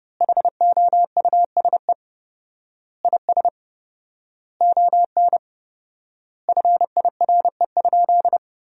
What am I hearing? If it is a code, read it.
Morse code, heard as a keyed tone, about 32 wpm, 700 Hz. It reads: HOUSE IS ON FIRE?
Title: 4x Word-Spacing - Sentences from Top 400 Words - 30wpm